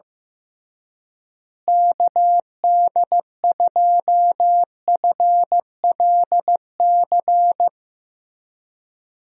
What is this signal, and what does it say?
Morse code, heard as a keyed tone, about 15 wpm, 700 Hz. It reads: KD2FLC
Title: USA Call Signs - 6 Characters - Repeated 2x - 15wpm